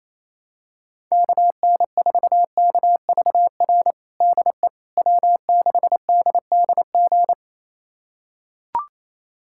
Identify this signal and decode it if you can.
Morse code, heard as a keyed tone, about 28 wpm, 700 Hz. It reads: KN4KVR DE W6DDG
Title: USA Callsign DE Callsign 28wpm